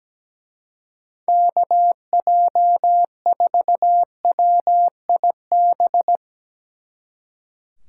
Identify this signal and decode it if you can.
Morse code, heard as a keyed tone, about 17 wpm, 700 Hz. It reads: KJ4WIB
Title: Speed Racing - USA 2x3 Call Signs 17wpm